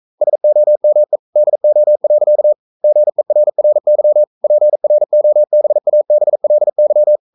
Morse code, heard as a keyed tone, about 42 wpm, 600 Hz. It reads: SOME DO. GERRY PROBABLY
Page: Romanen Never af Ken Follet på engelsk.